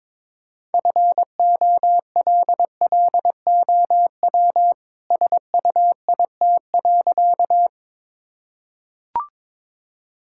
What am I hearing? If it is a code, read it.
Morse code, heard as a keyed tone, about 22 wpm, 700 Hz. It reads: FOLLOW SUIT.
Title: Sentences from Top 1600 Words 22wpm